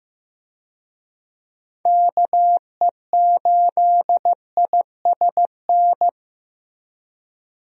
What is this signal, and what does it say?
Morse code, heard as a keyed tone, about 15 wpm, 700 Hz. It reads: KE8ISN
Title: USA Call Signs - 6 Characters - Repeated 2x - 15wpm